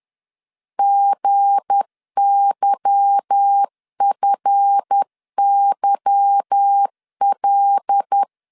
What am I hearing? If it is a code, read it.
Morse code, heard as a keyed tone, about 10 wpm, 800 Hz. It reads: GYFYL
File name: LEZIONI / 8 YJ45 / 8 Y 10wpm 2615